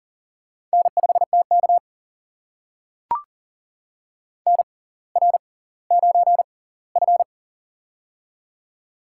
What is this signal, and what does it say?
Morse code, heard as a keyed tone, about 40 wpm, spaced as for 21 wpm, 690 Hz. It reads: N5TK NR9F
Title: USA Call Signs - 4 Characters - 6x Character Spacing - 40wpm